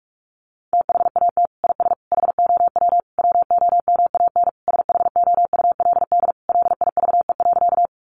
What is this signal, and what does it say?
Morse code, heard as a keyed tone, about 44 wpm, 700 Hz. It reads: THAT IS HOW WOMAN SHOULD LIVE.
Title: Sentences from Top 1100 Words 45wpm